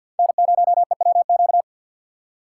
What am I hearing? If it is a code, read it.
Morse code, heard as a keyed tone, about 52 wpm, 700 Hz. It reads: N0EWQ
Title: USA Call Signs - 5 Characters - 3x Character Spacing - 50wpm